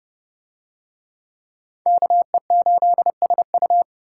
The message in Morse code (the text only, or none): KE8SU